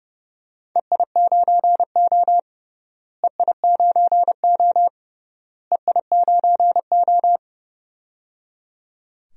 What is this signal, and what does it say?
Morse code, heard as a keyed tone, about 29 wpm, 700 Hz. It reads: EI9O EI9O EI9O
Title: DXCC-100 Call Signs - 4 Characters - Repeated 3x - 30wpm